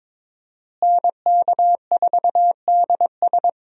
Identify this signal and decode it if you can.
Morse code, heard as a keyed tone, about 22 wpm, 700 Hz. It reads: NK4DS
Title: USA Call Signs - 5 Characters - Repeated 2x - 22wpm